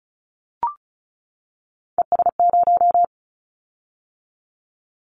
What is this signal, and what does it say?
Morse code, heard as a keyed tone, about 34 wpm, 700 Hz. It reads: ES0
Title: Callsign Prefixes for Top 100 DXCC - 35wpm